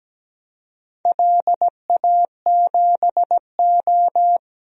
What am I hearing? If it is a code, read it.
Morse code, heard as a keyed tone, about 17 wpm, 700 Hz. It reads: LA7O
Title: DXCC-100 Call Signs - 4 Characters - 17wpm